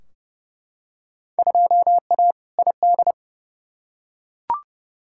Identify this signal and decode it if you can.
Morse code, heard as a keyed tone, about 29 wpm, 700 Hz. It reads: 2A ID
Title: Field Day Exchanges - 30wpm